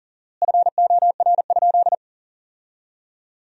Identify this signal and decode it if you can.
Morse code, heard as a keyed tone, about 40 wpm, 700 Hz. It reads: FOR?
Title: 4x Word-Spacing - Sentences from Top 100 Words - 40wpm